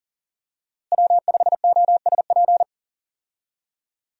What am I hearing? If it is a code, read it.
Morse code, heard as a keyed tone, about 40 wpm, 700 Hz. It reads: W5OSP
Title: USA Call Signs - 5 Characters - Repeated 3x - 40wpm